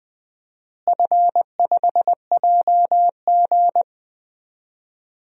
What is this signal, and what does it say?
Morse code, heard as a keyed tone, about 20 wpm, 700 Hz. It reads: F5JG
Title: DXCC-100 Call Signs - 4 Characters - Repeated 2x - 20wpm